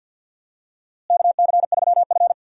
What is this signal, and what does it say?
Morse code, heard as a keyed tone, about 47 wpm, 700 Hz. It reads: KC3F
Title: USA Call Signs - 4 Characters - Repeated 3x - 50wpm